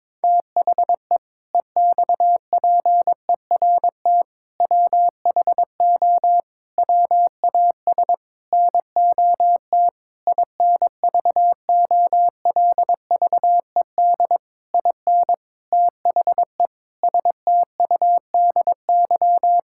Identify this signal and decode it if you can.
Morse code, heard as a keyed tone, about 22 wpm, 700 Hz. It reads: THE EXPERT WHO WAS NOT INVOLVED IN THE STUDY
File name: Sentences with 7-Word Phrases - 22wpm